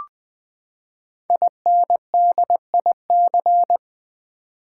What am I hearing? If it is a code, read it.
Morse code, heard as a keyed tone, about 20 wpm, 700 Hz. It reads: INDIC
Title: Build A Word - 5 Letters At A Time - 10-Letter Words - 20wpm